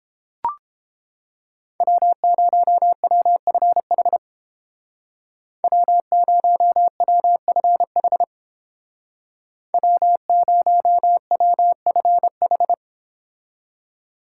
Morse code, heard as a keyed tone, about 30 wpm, 700 Hz. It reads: W0WFH W0WFH W0WFH